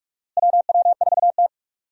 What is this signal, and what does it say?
Morse code, heard as a keyed tone, about 42 wpm, 700 Hz. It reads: WW4T
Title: USA Call Signs - 4 Characters - Repeated 3x - 45wpm